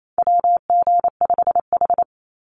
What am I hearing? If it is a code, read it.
Morse code, heard as a keyed tone, about 28 wpm, 700 Hz. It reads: WG5H
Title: Call Signs Repeated 3x - 28wpm